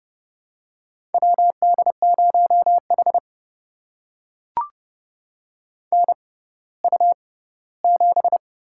Morse code, heard as a keyed tone, about 30 wpm, spaced as for 17 wpm, 700 Hz. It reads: WD0H NU7
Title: USA Call Signs - 4 Characters - 6x Character Spacing - 30wpm